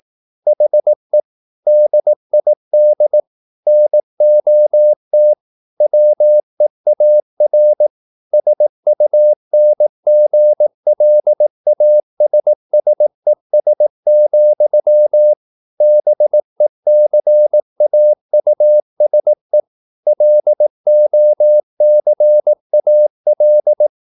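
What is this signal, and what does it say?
Morse code, heard as a keyed tone, about 18 wpm, 600 Hz. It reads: HE DID NOT WEAR SUNGLASSES, BECAUSE LOCAL
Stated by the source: Morse practice at 18 wpm, English text, from CW med Gnister.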